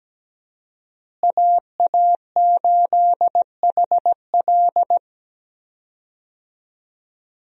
Morse code, heard as a keyed tone, about 17 wpm, 700 Hz. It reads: AA8HL
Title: USA Call Signs - 5 Characters - Repeated 3x - 17wpm